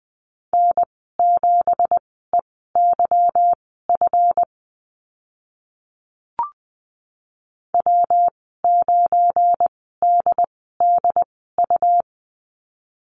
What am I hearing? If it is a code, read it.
Morse code, heard as a keyed tone, about 20 wpm, spaced as for 14 wpm, 700 Hz. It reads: N7EYF W9DDU
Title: 186 Lesson 30g - Letter b - Call Signs 20wpm